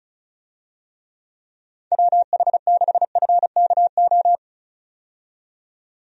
Morse code, heard as a keyed tone, about 36 wpm, 700 Hz. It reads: WH6FKO
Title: USA Call Signs - 6 Characters - 35wpm